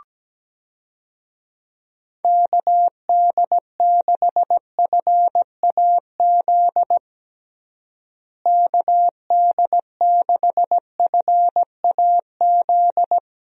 Morse code, heard as a keyed tone, about 17 wpm, 700 Hz. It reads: KD6FAZ KD6FAZ
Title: USA Call Signs - 6 Characters - Repeated 2x - 17wpm